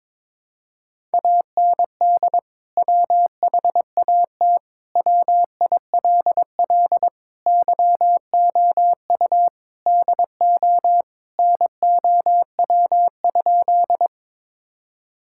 Morse code, heard as a keyed tone, about 21 wpm, 700 Hz. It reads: AND WHAT WILL YOU DO NOW?
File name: practice / 22wpm / Sentences with 2-Word Phrases - 22wpm